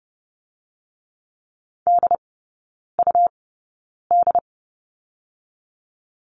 Morse code, heard as a keyed tone, about 30 wpm, spaced as for 9 wpm, 700 Hz. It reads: DUD
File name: Lesson 17e - Letter u - Three Characters 30wpm